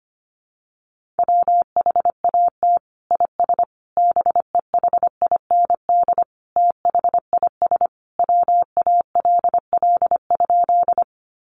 Morse code, heard as a keyed tone, about 25 wpm, 700 Hz. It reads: WHAT IS BEHIND THIS WALL?
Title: Sentences from Top 1100 Words 25wpm